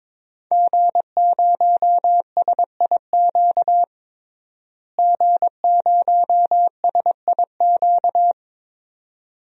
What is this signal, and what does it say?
Morse code, heard as a keyed tone, about 22 wpm, 700 Hz. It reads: G0SIQ G0SIQ
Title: DXCC-100 Call Signs - 5 Characters - Repeated 2x - 22wpm